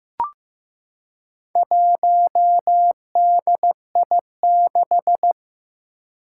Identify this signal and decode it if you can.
Morse code, heard as a keyed tone, about 15 wpm, 700 Hz. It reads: 1DI6
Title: Four Challenging Letter-Number Combinations 15wpm